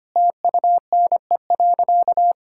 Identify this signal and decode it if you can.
Morse code, heard as a keyed tone, about 25 wpm, 700 Hz. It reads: TUNE.
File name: Easy Sentences with Hints - 25wpm